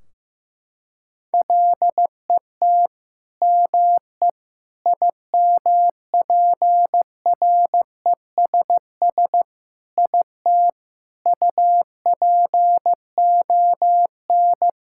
Morse code, heard as a keyed tone, about 16 wpm, 700 Hz. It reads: LET ME IMPRESS IT UPON
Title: Sentences from Top 1600 Words 15wpm